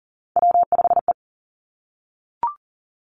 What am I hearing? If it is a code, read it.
Morse code, heard as a keyed tone, about 41 wpm, 700 Hz. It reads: W5E
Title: Three Challenging Letter-Number Combinations 40wpm